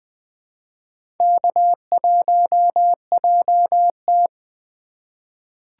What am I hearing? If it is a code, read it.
Morse code, heard as a keyed tone, about 20 wpm, 700 Hz. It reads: K1JT
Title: USA Call Signs - 4x Character Spacing - 20wpm